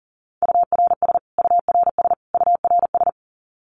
Morse code, heard as a keyed tone, about 40 wpm, 700 Hz. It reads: URS URS URS
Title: Abbreviations Repeated 3x - 40wpm